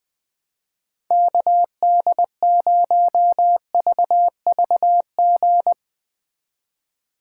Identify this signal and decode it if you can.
Morse code, heard as a keyed tone, about 20 wpm, 700 Hz. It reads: KD0VVG
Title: USA Call Signs - 6 Characters - 4x Character Spacing - 20wpm